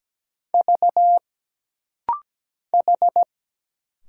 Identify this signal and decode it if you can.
Morse code, heard as a keyed tone, about 17 wpm, 700 Hz. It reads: V H
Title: Single Letter-Number Alt - Rapid-Fire 17wpm